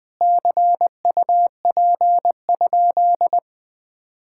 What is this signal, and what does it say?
Morse code, heard as a keyed tone, about 20 wpm, 700 Hz. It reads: CUP?
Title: Sentences from Top 1600 Words 20wpm